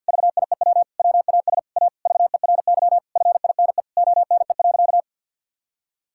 Morse code, heard as a keyed tone, about 50 wpm, 700 Hz. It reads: VIEW WAS A VERY FINE ONE.
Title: Sentences from Top 1000 Words - Review Part 6 - 50wpm